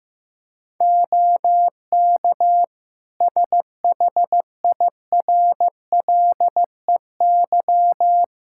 Morse code, heard as a keyed tone, about 16 wpm, 700 Hz. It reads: OK SHIRLEY
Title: Practice QSO - Exchange 3 - More Info - 15wpm